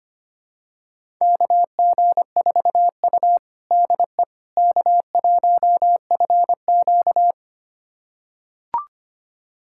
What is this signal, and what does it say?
Morse code, heard as a keyed tone, about 25 wpm, 700 Hz. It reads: KG4U DE K1FQ
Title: USA Callsign DE Callsign - 4 Characters - 25wpm